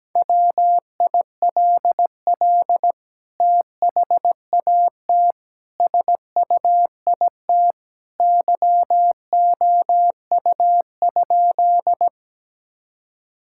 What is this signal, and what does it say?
Morse code, heard as a keyed tone, about 17 wpm, 700 Hz. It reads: WILL THAT SUIT YOU?
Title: Sentences from Top 1600 Words 17wpm